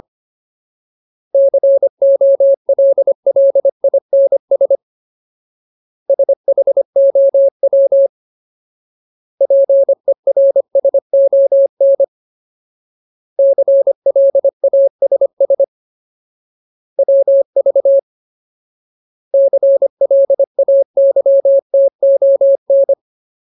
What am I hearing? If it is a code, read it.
Morse code, heard as a keyed tone, about 25 wpm, 550 Hz. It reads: COLLINS SHOW PERSON CLASS WV CLAYTON
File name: Ultimate Word List / 25WPM Ultimate